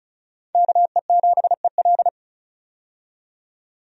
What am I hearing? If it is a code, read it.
Morse code, heard as a keyed tone, about 36 wpm, 700 Hz. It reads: KE7EL
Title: USA Call Signs - 5 Characters - 35wpm